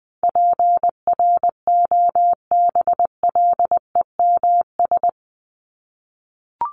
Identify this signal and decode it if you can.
Morse code, heard as a keyed tone, about 20 wpm, 700 Hz. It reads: PROBLEMS
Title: ES:S Suffix - Top 100 Words with Base Words 20wpm